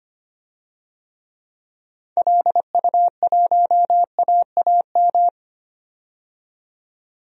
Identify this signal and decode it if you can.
Morse code, heard as a keyed tone, about 25 wpm, 700 Hz. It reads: LU1AAM
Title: DXCC-100 Call Signs - 6 Characters - 25wpm